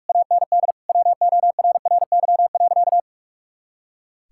Morse code, heard as a keyed tone, about 41 wpm, 700 Hz. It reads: AND WORRY.
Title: Pangram Challenge v2 - 45wpm